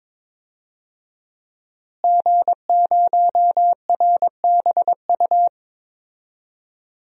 Morse code, heard as a keyed tone, about 22 wpm, 700 Hz. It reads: G0RBU
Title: DXCC-100 Call Signs - 5 Characters - Repeated 3x - 22wpm